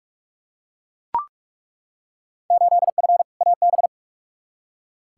Morse code, heard as a keyed tone, about 42 wpm, 700 Hz. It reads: 8F AB
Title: Field Day Exchanges Encore - 45wpm